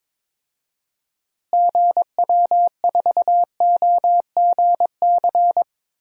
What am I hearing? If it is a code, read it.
Morse code, heard as a keyed tone, about 21 wpm, 700 Hz. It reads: GW4OGC
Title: DXCC-100 Call Signs - 6 Characters - Repeated 3x - 22wpm